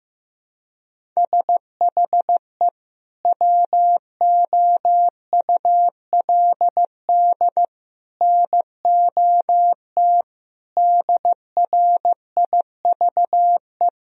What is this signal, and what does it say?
Morse code, heard as a keyed tone, about 15 wpm, 700 Hz. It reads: SHE WOULD NOT DRIVE
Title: Sentences from Top 1000 Words - Review Part 3 - 15wpm